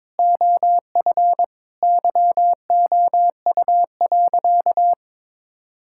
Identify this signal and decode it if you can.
Morse code, heard as a keyed tone, about 21 wpm, 700 Hz. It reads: OF YOU.
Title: Sentences from Top 1000 Words - Review Part 5 - 22wpm